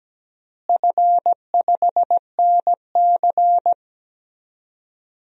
Morse code, heard as a keyed tone, about 17 wpm, 700 Hz. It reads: F5NC
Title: DXCC-100 Call Signs - 4 Characters - Repeated 3x - 17wpm